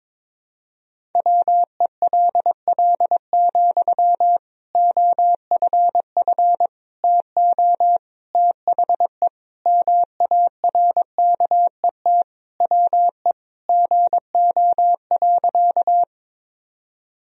Morse code, heard as a keyed tone, about 22 wpm, 700 Hz. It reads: WELL, OFF TO THE MARKET WE GO.
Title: Sentences from Top 1000 Words - Review Part 3 - 22wpm